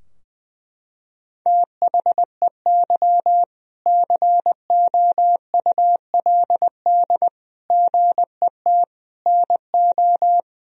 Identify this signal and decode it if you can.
Morse code, heard as a keyed tone, about 20 wpm, 700 Hz. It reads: THEY COULD GET NO
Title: Sentences from Top 1000 Words - Review Part 3 - 20wpm